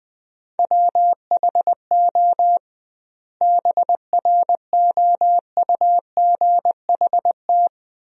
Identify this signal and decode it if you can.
Morse code, heard as a keyed tone, about 21 wpm, 700 Hz. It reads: WHO BROUGHT
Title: 2x Word-Spacing - Sentences from Top 500 Words - 20wpm